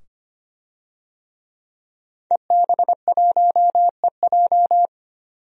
Morse code, heard as a keyed tone, about 25 wpm, 700 Hz. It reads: EB1EJ
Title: DXCC-100 Call Signs - 5 Characters - 25wpm